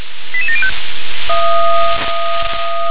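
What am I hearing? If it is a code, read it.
Five-tone selective call (ZVEI), heard as quick runs of five tones, tones of about 70 ms. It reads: 88705
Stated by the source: Received on an SDR; ZVEI five-tone call 88694.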